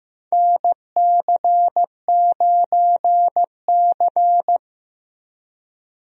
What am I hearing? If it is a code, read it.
Morse code, heard as a keyed tone, about 15 wpm, 700 Hz. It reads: NC9C
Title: USA Call Signs - 4 Characters - Repeated 3x - 15wpm